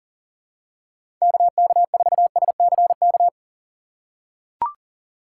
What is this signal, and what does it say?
Morse code, heard as a keyed tone, about 40 wpm, 700 Hz. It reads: KK4SCK
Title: USA Call Signs - 6 Characters - 3x Character Spacing - 40wpm